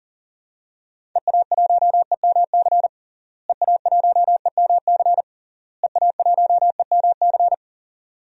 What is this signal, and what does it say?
Morse code, heard as a keyed tone, about 37 wpm, 700 Hz. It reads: EA1EMC EA1EMC EA1EMC
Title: DXCC-100 Call Signs - 6 Characters - Repeated 3x - 40wpm